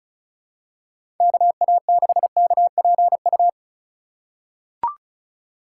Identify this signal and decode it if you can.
Morse code, heard as a keyed tone, about 36 wpm, 700 Hz. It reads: KA6KPU
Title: USA Call Signs - 3x Character Spacing - 35wpm